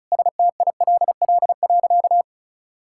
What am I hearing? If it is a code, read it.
Morse code, heard as a keyed tone, about 36 wpm, 700 Hz. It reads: STILL.
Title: Sentences from Top 1000 Words - Review Part 6 - 35wpm